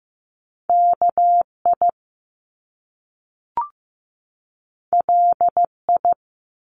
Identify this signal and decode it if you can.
Morse code, heard as a keyed tone, about 15 wpm, 700 Hz. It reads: KI LI
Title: Top Two Letter Combinations in Words 15wpm